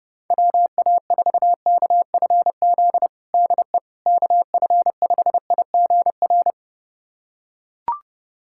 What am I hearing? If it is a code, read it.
Morse code, heard as a keyed tone, about 32 wpm, 700 Hz. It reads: WA4KFZ DE KF5IGR
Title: USA Callsign DE Callsign - 6 Characters - 30wpm